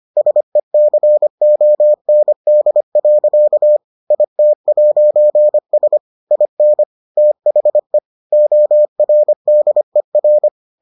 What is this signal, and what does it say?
Morse code, heard as a keyed tone, about 25 wpm, 600 Hz. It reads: SECOND. IT'S IN THE ORDER